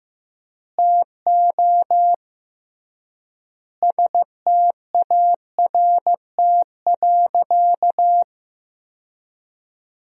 Morse code, heard as a keyed tone, about 15 wpm, 700 Hz. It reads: TO START.
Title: 3x Word-Spacing - Sentences from Top 300 Words - 15wpm